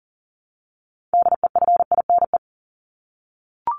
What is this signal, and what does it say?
Morse code, heard as a keyed tone, about 40 wpm, 700 Hz. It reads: DEFINE
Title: ED Suffix - Top 100 Words with Base Words 40wpm